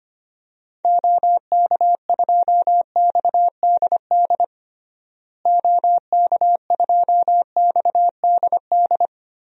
Morse code, heard as a keyed tone, about 25 wpm, 700 Hz. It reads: OK2XDD OK2XDD
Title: DXCC-100 Call Signs - 6 Characters - Repeated 2x - 25wpm